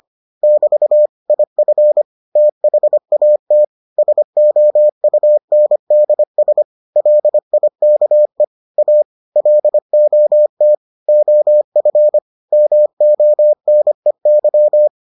Morse code, heard as a keyed tone, about 25 wpm, 600 Hz. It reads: <BT> IF THAT SOUNDS LIKE A LOT OF MONEY